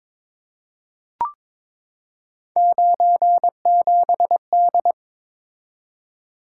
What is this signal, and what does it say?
Morse code, heard as a keyed tone, about 22 wpm, 700 Hz. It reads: 97D
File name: practice / 22wpm / Three Letter-Number Combinations - 22wpm